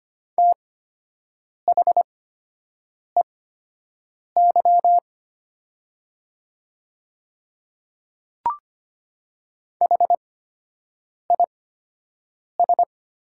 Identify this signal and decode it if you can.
Morse code, heard as a keyed tone, about 25 wpm, spaced as for 7 wpm, 700 Hz. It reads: THEY HIS
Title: 8x Character Spacing - Top 100 Words - 25wpm